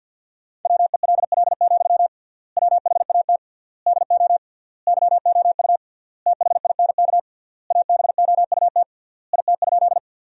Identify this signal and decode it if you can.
Morse code, heard as a keyed tone, about 50 wpm, 700 Hz. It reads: WELL, WHAT DO YOU THINK ABOUT IT?
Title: 3x Word-Spacing - Sentences from Top 200 Words - 50wpm